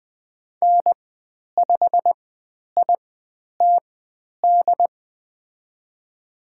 Morse code, heard as a keyed tone, about 20 wpm, spaced as for 10 wpm, 700 Hz. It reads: N5ITD
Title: 109 Lesson 18g - Letter m - Call Signs 20wpm